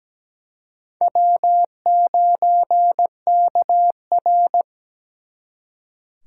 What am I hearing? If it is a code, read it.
Morse code, heard as a keyed tone, about 20 wpm, 700 Hz. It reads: W9KR
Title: Speed Racing - USA 1x2 Call Signs 17wpm